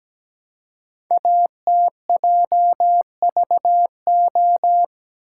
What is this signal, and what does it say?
Morse code, heard as a keyed tone, about 17 wpm, 700 Hz. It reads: ATJVO
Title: Code Groups - 5 Letters - 17wpm